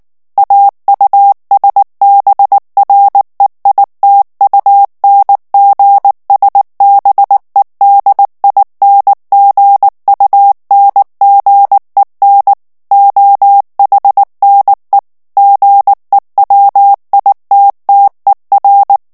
Morse code, heard as a keyed tone, about 19 wpm, 800 Hz. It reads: AUSBREITUNGSBEDINGUNGEN OHNE GEWITTER
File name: morse mp3 / ALLEMAND texte clair 2 vitesse 18 mots minute